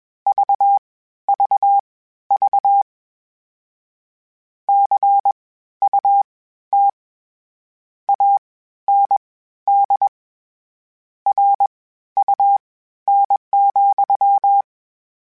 Morse code, heard as a keyed tone, about 21 wpm, spaced as for 13 wpm, 800 Hz. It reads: VVV CUT AND RUN,
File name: Misc / Misc-mp3s / WordPairs sp @20